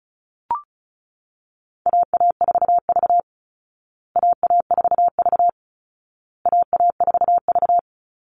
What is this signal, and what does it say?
Morse code, heard as a keyed tone, about 34 wpm, 700 Hz. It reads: AA4V AA4V AA4V
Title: Call Signs Repeated 2x - 35wpm